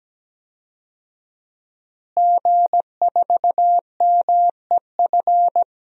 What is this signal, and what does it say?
Morse code, heard as a keyed tone, about 17 wpm, 700 Hz. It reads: G4MEF
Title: DXCC-100 Call Signs - 5 Characters - Repeated 2x - 17wpm